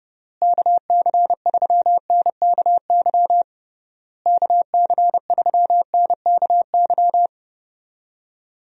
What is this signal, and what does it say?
Morse code, heard as a keyed tone, about 30 wpm, 700 Hz. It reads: KC3NKY KC3NKY
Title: USA Call Signs - 6 Characters - Repeated 3x - 30wpm